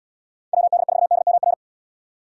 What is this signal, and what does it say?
Morse code, heard as a keyed tone, about 68 wpm, 700 Hz. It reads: WD4DMR